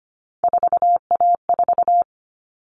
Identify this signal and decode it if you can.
Morse code, heard as a keyed tone, about 25 wpm, 700 Hz. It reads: 4A4
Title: All Cut Numbers - 3-Digit 25wpm